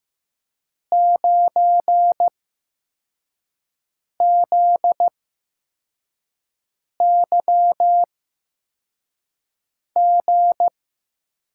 Morse code, heard as a keyed tone, about 15 wpm, spaced as for 4 wpm, 700 Hz. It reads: 9ZYG
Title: USA Call Signs - 8x Character Spacing - 15wpm